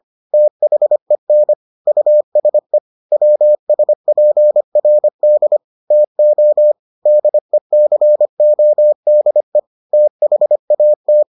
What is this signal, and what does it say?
Morse code, heard as a keyed tone, about 25 wpm, 600 Hz. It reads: THEN USE WSPRD TO DECODE THAT